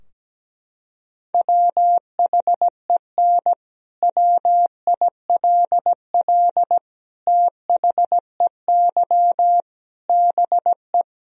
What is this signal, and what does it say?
Morse code, heard as a keyed tone, about 17 wpm, 700 Hz. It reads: WHEN WILL THEY BE
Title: Sentences from Top 1000 Words - Review Part 3 - 17wpm